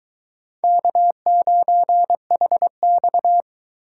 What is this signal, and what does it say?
Morse code, heard as a keyed tone, about 22 wpm, 700 Hz. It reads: K9HX